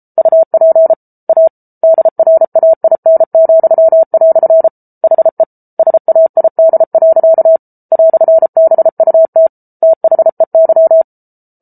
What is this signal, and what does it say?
Morse code, heard as a keyed tone, about 34 wpm, 650 Hz. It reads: UP A DRAIN," HE SAID. "BUT THEY